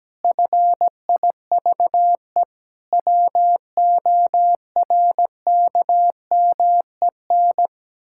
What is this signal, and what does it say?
Morse code, heard as a keyed tone, about 17 wpm, 700 Hz. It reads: FIVE WORKMEN
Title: Pangram Challenge v2 - 17wpm